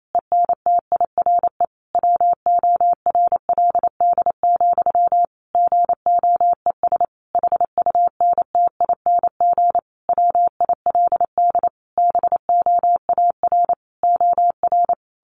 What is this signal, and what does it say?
Morse code, heard as a keyed tone, about 28 wpm, 700 Hz. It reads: ENTIRE WORLD, GOES HUNTING WILD BOAR OR
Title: A Sound Of Thunder 28wpm